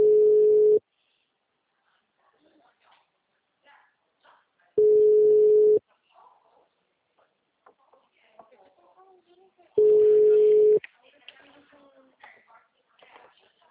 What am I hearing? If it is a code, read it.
Ringback tone (Europe, 425 Hz), with speech.
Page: Atibaia - SP